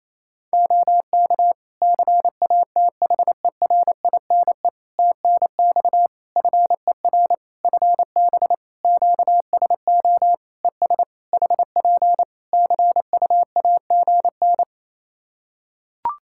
Morse code, heard as a keyed tone, about 28 wpm, 700 Hz. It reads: OK CATHERINE TNX FER FB QSO ES HP CUAGN
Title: Practice QSO - Exchange 4 - Ending - 28wpm